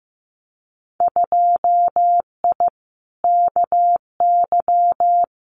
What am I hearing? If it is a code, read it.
Morse code, heard as a keyed tone, about 15 wpm, 700 Hz. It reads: 2I KY
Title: Winter Field Day Exchanges 15wpm